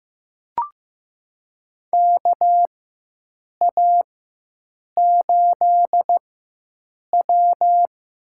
Morse code, heard as a keyed tone, about 16 wpm, spaced as for 7 wpm, 700 Hz. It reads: KA8W
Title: USA Call Signs - 6 Characters - 4x Character Spacing - 15wpm